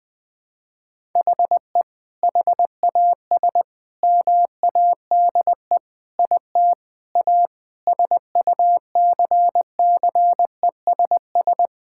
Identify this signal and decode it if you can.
Morse code, heard as a keyed tone, about 20 wpm, 700 Hz. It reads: HE HAS MADE IT A SUCCESS
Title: Sentences from Top 1000 Words - Review Part 6 - 20wpm